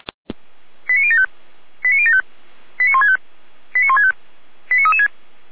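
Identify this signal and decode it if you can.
Five-tone selective call (ZVEI), heard as quick runs of five tones, tones of about 80 ms. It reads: 89075 89075 89165 89165 89336
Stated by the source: ZVEI five-tone selective call 89073 received by SDR, from fire brigade Orlamünde (feuerwehr-orlamuende) server.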